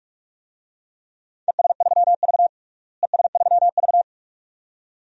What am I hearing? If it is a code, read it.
Morse code, heard as a keyed tone, about 44 wpm, 700 Hz. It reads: ES3V ES3V
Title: DXCC-100 Call Signs - 4 Characters - Repeated 2x - 45wpm